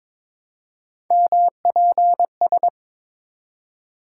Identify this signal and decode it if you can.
Morse code, heard as a keyed tone, about 22 wpm, 700 Hz. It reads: MPS
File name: practice / 22wpm / Three Letter-Number Combinations - 22wpm